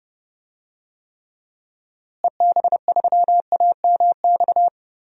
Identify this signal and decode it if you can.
Morse code, heard as a keyed tone, about 32 wpm, 700 Hz. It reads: EB3AMX
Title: DXCC-100 Call Signs - 6 Characters - Repeated 3x - 30wpm